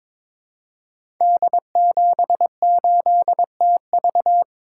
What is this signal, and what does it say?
Morse code, heard as a keyed tone, about 22 wpm, 690 Hz. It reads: D78TV
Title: Code Groups - 5 Letter:Numbers - 22wpm